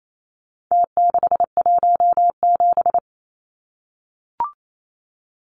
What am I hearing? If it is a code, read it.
Morse code, heard as a keyed tone, about 28 wpm, 700 Hz. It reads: T617
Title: Numbers - Contest Style - 4-Digit 28wpm